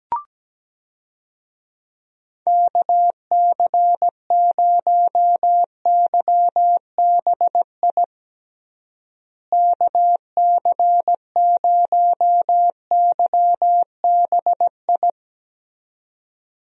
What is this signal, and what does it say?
Morse code, heard as a keyed tone, about 17 wpm, 700 Hz. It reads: KC0YBI KC0YBI
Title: USA Call Signs - 6 Characters - Repeated 2x - 17wpm